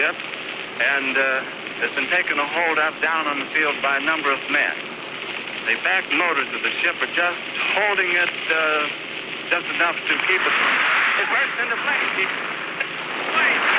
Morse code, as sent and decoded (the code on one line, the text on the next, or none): none